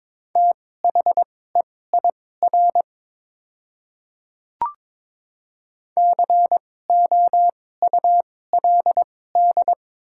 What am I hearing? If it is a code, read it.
Morse code, heard as a keyed tone, about 22 wpm, spaced as for 16 wpm, 700 Hz. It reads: THEIR COULD
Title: 2x Character Spacing - Top 500 Words - 22wpm